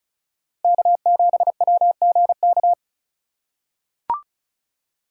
Morse code, heard as a keyed tone, about 36 wpm, 700 Hz. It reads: K7WGK
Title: USA Call Signs - 5 Characters - 4x Character Spacing - 35wpm